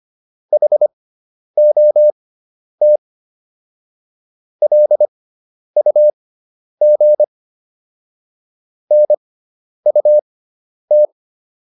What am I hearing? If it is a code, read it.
Morse code, heard as a keyed tone, about 25 wpm, spaced as for 10 wpm, 600 Hz. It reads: HOT LUG NUT